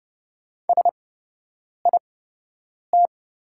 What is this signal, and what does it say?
Morse code, heard as a keyed tone, about 30 wpm, spaced as for 8 wpm, 700 Hz. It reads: SIT
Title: 8x Character Spacing - Top 500 Words - 30wpm